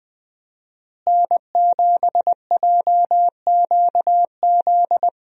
Morse code, heard as a keyed tone, about 20 wpm, 700 Hz. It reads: N7JQZ
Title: USA Call Signs - 5 Characters - Repeated 3x - 20wpm